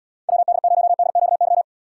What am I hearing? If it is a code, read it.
Morse code, heard as a keyed tone, about 78 wpm, 700 Hz. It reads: WD0DQY